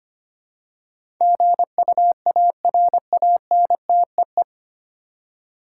Morse code, heard as a keyed tone, about 25 wpm, 700 Hz. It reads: GUARANTEE
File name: Nine Letter Words - 25wpm